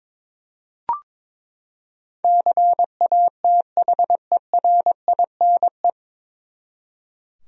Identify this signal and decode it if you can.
Morse code, heard as a keyed tone, about 22 wpm, 700 Hz. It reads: CATHERINE
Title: USA Common Names - 22wpm